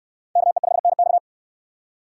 Morse code, heard as a keyed tone, about 66 wpm, 700 Hz. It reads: KE5NX